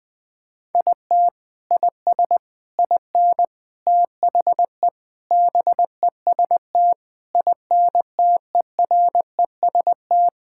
Morse code, heard as a keyed tone, about 20 wpm, 700 Hz. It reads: IT IS IN THE BEST INTEREST
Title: Sentences with 6-Word Phrases - 20wpm